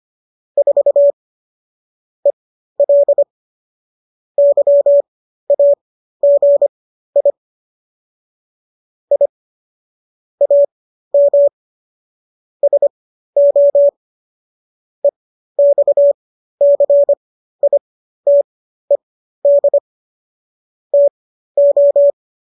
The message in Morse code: 4 EL YAGI I AM SO EXCITED TO

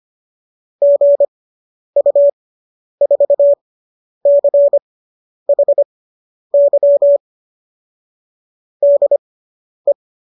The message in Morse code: GU4CHY DE

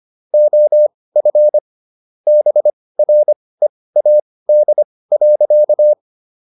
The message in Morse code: OF BREAD.

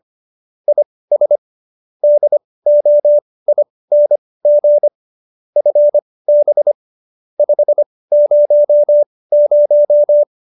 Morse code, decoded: IS DOING FB 500

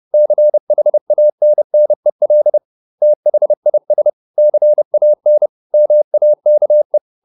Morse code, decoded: CHANNEL THIS CAN MAKE

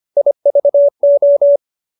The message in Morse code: IVO

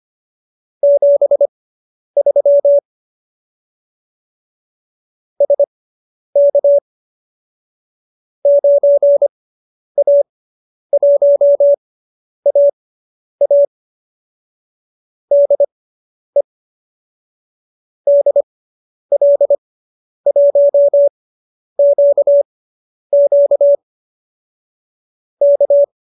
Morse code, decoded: 73 SK 9A1AA DE DL1QQ K

